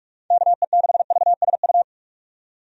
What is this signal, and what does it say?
Morse code, heard as a keyed tone, about 41 wpm, 700 Hz. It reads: KE6VSU
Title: USA Call Signs - 6 Characters - 45wpm